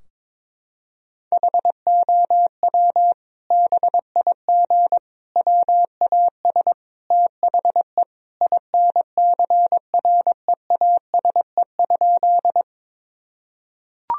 Morse code, heard as a keyed tone, about 22 wpm, 700 Hz. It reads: HOW BIG WAS THE INCREASE?
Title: Sentences from Top 1000 Words - Review Part 4 - 22wpm